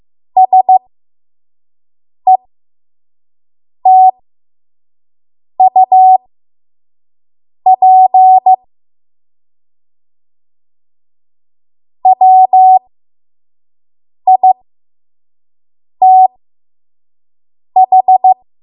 Morse code, heard as a keyed tone, about 15 wpm, spaced as for 5 wpm, 750 Hz. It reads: SETUP WITH
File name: Morse / 05WPM